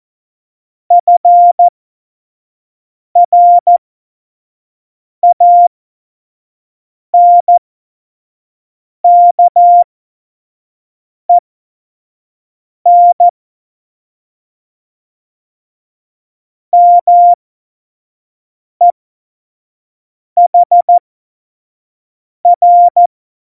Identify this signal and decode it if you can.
Morse code, heard as a keyed tone, about 14 wpm, spaced as for 5 wpm, 700 Hz. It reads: FRANKEN MEHR
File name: Morsekurs / Witze / Witz 13. August 2019, 25 F70